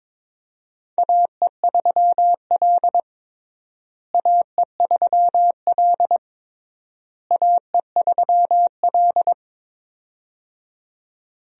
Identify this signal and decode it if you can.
Morse code, heard as a keyed tone, about 22 wpm, 700 Hz. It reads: AE3L AE3L AE3L
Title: USA Call Signs - 4 Characters - Repeated 3x - 22wpm